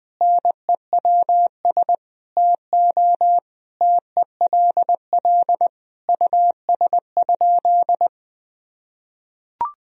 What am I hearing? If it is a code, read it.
Morse code, heard as a keyed tone, about 21 wpm, 700 Hz. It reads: NEWS TO TELL US?
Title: Sentences from Top 1000 Words - Review Part 4 - 20wpm